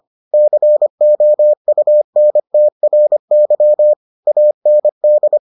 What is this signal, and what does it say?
Morse code, heard as a keyed tone, about 25 wpm, 600 Hz. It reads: COUNTRY AND